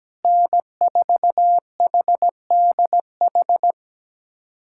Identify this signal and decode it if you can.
Morse code, heard as a keyed tone, about 17 wpm, 700 Hz. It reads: N4HDH
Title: USA Callsign DE Callsign - 5 Characters - 17wpm